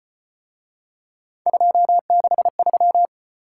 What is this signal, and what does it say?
Morse code, heard as a keyed tone, about 34 wpm, 700 Hz. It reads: 263